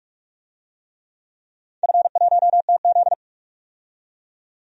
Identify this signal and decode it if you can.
Morse code, heard as a keyed tone, about 46 wpm, 700 Hz. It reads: F1TZ
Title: DXCC-100 Call Signs - 4 Characters - 45wpm